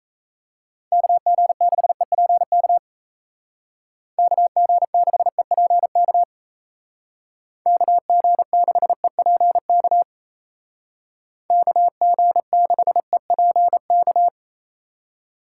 Morse code, heard as a keyed tone, about 33 wpm, 700 Hz. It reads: KG6EPK KG6EPK KG6EPK KG6EPK
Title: Speed Racing - USA 2x3 Call Signs 28wpm